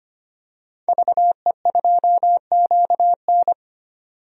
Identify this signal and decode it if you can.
Morse code, heard as a keyed tone, about 25 wpm, 700 Hz. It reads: VE2QN
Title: DXCC-100 Call Signs - 5 Characters - 25wpm